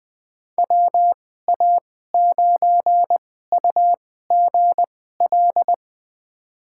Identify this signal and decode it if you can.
Morse code, heard as a keyed tone, about 20 wpm, spaced as for 14 wpm, 700 Hz. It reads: WA9UGL
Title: USA Call Signs - 6 Characters - 2x Character Spacing - 20wpm